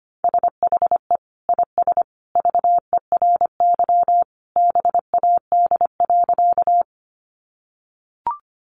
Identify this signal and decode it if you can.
Morse code, heard as a keyed tone, about 25 wpm, 700 Hz. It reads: SHE IS VERY BAD.
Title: Sentences from Top 700 Words 25wpm